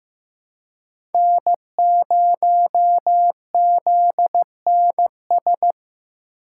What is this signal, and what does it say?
Morse code, heard as a keyed tone, about 15 wpm, 700 Hz. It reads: N0ZNS
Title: USA Call Signs - 5 Characters - 4x Character Spacing - 15wpm